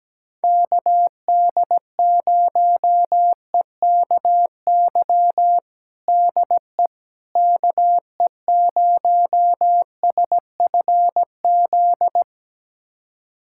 Morse code, heard as a keyed tone, about 17 wpm, 700 Hz. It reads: KD0EKY DE KE0SFZ
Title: USA Callsign DE Callsign - 6 Characters - 17wpm